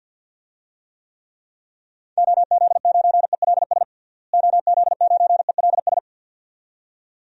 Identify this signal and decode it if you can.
Morse code, heard as a keyed tone, about 46 wpm, 700 Hz. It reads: OZ9ELS OZ9ELS
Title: DXCC-100 Call Signs - 6 Characters - Repeated 2x - 50wpm